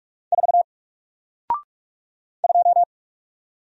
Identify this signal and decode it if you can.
Morse code, heard as a keyed tone, about 44 wpm, 700 Hz. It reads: 4 2
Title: Single Letter-Number Alt 45wpm